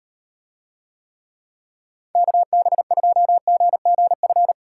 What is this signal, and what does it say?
Morse code, heard as a keyed tone, about 38 wpm, 700 Hz. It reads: KB2GGF